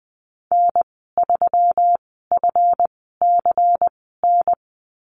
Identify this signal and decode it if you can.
Morse code, heard as a keyed tone, about 20 wpm, spaced as for 14 wpm, 700 Hz. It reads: N3FCN
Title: 193 Lesson 31g - Letter v - Call Signs 20wpm